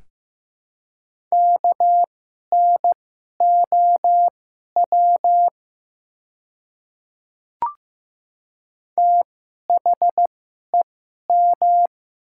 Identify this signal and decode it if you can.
Morse code, heard as a keyed tone, about 15 wpm, spaced as for 11 wpm, 700 Hz. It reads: KNOW THEM
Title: 2x Character Spacing - Top 100 Words - 15wpm